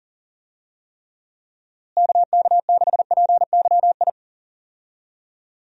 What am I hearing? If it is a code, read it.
Morse code, heard as a keyed tone, about 40 wpm, 700 Hz. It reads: KK6PYI
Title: USA Call Signs - 6 Characters - Repeated 3x - 40wpm